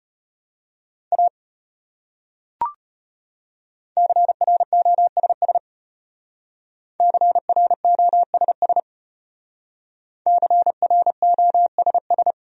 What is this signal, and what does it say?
Morse code, heard as a keyed tone, about 33 wpm, 700 Hz. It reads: A CROSS CROSS CROSS